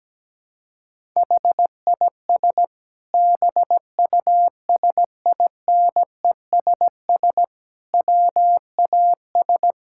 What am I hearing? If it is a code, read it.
Morse code, heard as a keyed tone, about 17 wpm, 700 Hz. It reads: HIS BUSINESS WAS
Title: Sentences from Top 1000 Words - Review Part 6 - 17wpm